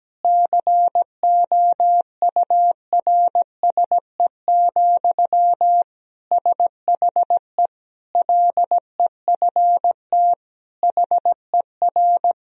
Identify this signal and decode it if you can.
Morse code, heard as a keyed tone, about 17 wpm, 700 Hz. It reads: COURSE, SHE LEFT HER
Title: Sentences from Top 1000 Words - Review Part 3 - 17wpm